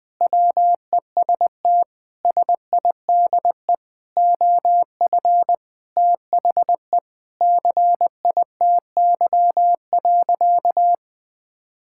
Morse code, heard as a keyed tone, about 20 wpm, 700 Hz. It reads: WEST SIDE OF THE CITY.
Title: Sentences with 6-Word Phrases - Spoken Beginning - 20wpm